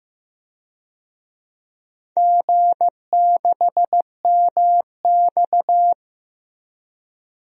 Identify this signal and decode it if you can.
Morse code, heard as a keyed tone, about 15 wpm, 700 Hz. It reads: G6MX
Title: DXCC-100 Call Signs - 4 Characters - Repeated 3x - 15wpm